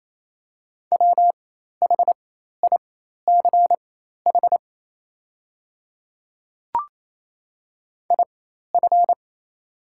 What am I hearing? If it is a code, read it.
Morse code, heard as a keyed tone, about 27 wpm, spaced as for 13 wpm, 700 Hz. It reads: WHICH IF
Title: 4x Character Spacing - Top 100 Words - 28wpm